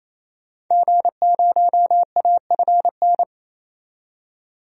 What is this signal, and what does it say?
Morse code, heard as a keyed tone, about 28 wpm, 700 Hz. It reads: G0AFN
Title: DXCC-100 Call Signs - 5 Characters - 28wpm